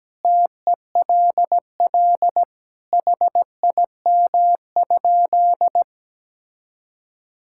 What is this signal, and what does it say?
Morse code, heard as a keyed tone, about 17 wpm, 700 Hz. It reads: TELL HIM?
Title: Sentences from Top 1000 Words - Review Part 5 - 17wpm